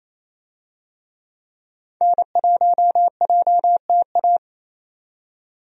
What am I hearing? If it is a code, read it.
Morse code, heard as a keyed tone, about 28 wpm, 700 Hz. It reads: N1JTA
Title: USA Call Signs - 5 Characters - 28wpm